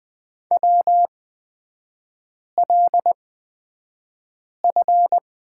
Morse code, heard as a keyed tone, about 20 wpm, spaced as for 5 wpm, 690 Hz. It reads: WLF
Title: 255 Lesson 41e - Period - Three Characters 20wpm